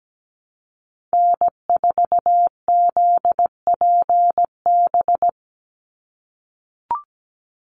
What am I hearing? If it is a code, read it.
Morse code, heard as a keyed tone, about 17 wpm, 700 Hz. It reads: N4ZPB
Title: USA Call Signs 17wpm